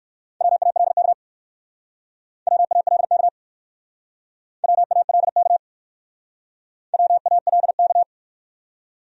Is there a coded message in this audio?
Yes: Morse code from 0.4 to 8.0 s, heard as a keyed tone, about 60 wpm, 700 Hz. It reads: WALK WALK WALK WALK